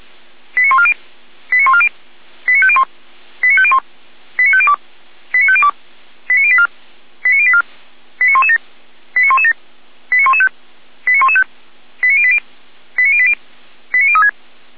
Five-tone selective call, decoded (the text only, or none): 89140 89140 89501 89501 89502 89502 89084 89084 89117 89117 89115 89115 89080 89080 89036